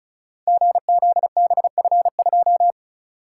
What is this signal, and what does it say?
Morse code, heard as a keyed tone, about 36 wpm, 690 Hz. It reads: GZBF2
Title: Code Groups - 5 Letter:Numbers - 35wpm